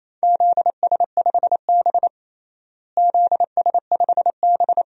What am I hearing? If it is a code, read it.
Morse code, heard as a keyed tone, about 28 wpm, 700 Hz. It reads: ZS5B ZS5B
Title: DXCC-100 Call Signs - 4 Characters - Repeated 2x - 28wpm